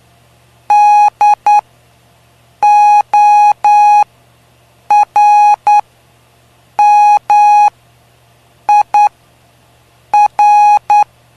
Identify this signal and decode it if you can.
Morse code, heard as a keyed tone, about 9 wpm, spaced as for 6 wpm, 830 Hz. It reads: DORMIR
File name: cw / py2lcd / 05-PY2LCD-Li��o 5